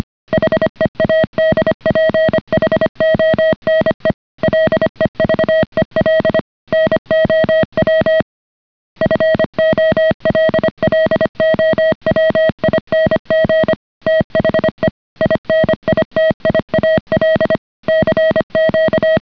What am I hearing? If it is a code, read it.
Morse code, heard as a keyed tone, about 25 wpm, 630 Hz. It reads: HEADPHONE LEVEL NOW FOLLOWING THE INITIAL CQ